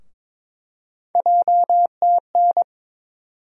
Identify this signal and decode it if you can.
Morse code, heard as a keyed tone, about 22 wpm, 700 Hz. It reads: JTN
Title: USA Call Sign Postfixes - 3 Letters 22wpm